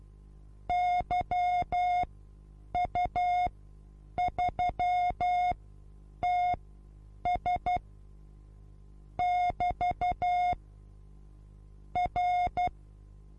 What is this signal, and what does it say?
Morse code, heard as a keyed tone, about 12 wpm, spaced as for 8 wpm, 720 Hz. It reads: YU3TS <BT> R
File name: Onlinelehrgang / morsen / DJ4UF Morsen23